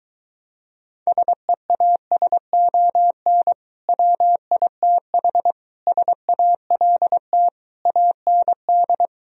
Morse code, heard as a keyed tone, about 23 wpm, 700 Hz. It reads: SEASON WITH SALT AND